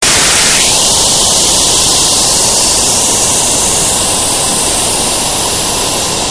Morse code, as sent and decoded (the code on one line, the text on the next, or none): none